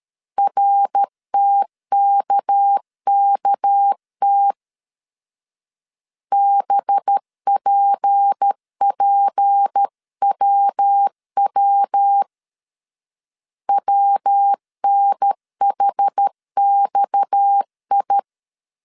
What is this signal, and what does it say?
Morse code, heard as a keyed tone, about 13 wpm, 800 Hz. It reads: RTKKT BPPWW WNHXI
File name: CW / DET / 1 6 1500C S B 12wpm 3431